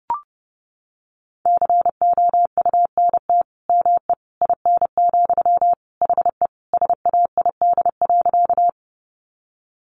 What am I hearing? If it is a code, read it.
Morse code, heard as a keyed tone, about 30 wpm, 700 Hz. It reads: COUNT ME IN, HE SAID.
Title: Sentences from Top 1500 Words 30wpm